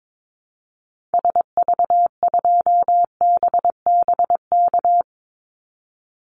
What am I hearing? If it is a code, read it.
Morse code, heard as a keyed tone, about 22 wpm, 700 Hz. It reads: SV2BBK
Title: Call Signs Repeated 2x - 22wpm